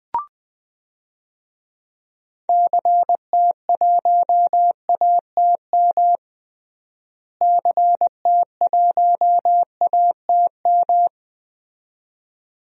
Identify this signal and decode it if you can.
Morse code, heard as a keyed tone, about 20 wpm, 700 Hz. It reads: CT1ATM CT1ATM
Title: DXCC-100 Call Signs - 6 Characters - Repeated 2x - 20wpm